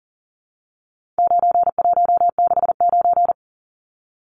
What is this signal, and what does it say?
Morse code, heard as a keyed tone, about 40 wpm, 700 Hz. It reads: 9169
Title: Numbers - 4-Digit 40wpm